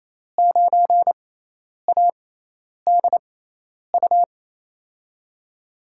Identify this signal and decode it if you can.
Morse code, heard as a keyed tone, about 28 wpm, spaced as for 10 wpm, 700 Hz. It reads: 9ADU
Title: USA Call Signs - 5 Characters - 6x Character Spacing - 28wpm